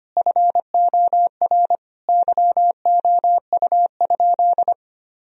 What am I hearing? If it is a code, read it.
Morse code, heard as a keyed tone, about 25 wpm, 700 Hz. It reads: FOR YOU?
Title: Sentences from Top 1000 Words - Review Part 2 - 25wpm